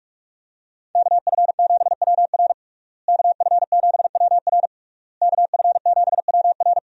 Morse code, heard as a keyed tone, about 41 wpm, 700 Hz. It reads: KF7WR KF7WR KF7WR
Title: USA Call Signs - 5 Characters - Repeated 3x - 45wpm